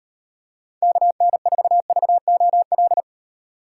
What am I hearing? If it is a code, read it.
Morse code, heard as a keyed tone, about 38 wpm, 700 Hz. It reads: KN4VOL